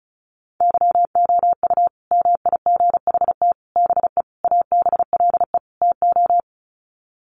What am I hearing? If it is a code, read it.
Morse code, heard as a keyed tone, about 34 wpm, 700 Hz. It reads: YOU MIGHT BE ABLE TO
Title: Sets of 5 Words Encore2 35wpm